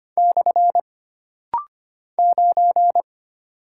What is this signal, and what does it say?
Morse code, heard as a keyed tone, about 25 wpm, 700 Hz. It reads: / 9
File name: Single Character (Letter:Number:Punctuation:Prosign) - Rapid-Fire - 25wpm